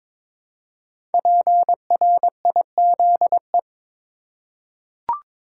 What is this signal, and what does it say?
Morse code, heard as a keyed tone, about 22 wpm, 700 Hz. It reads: PRIZE
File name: 5 Letter Words - First 3 Letters Given - 22wpm